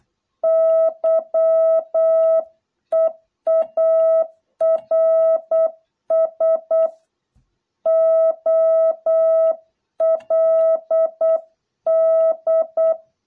Morse code, heard as a keyed tone, about 8 wpm, 660 Hz. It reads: YEARS OLD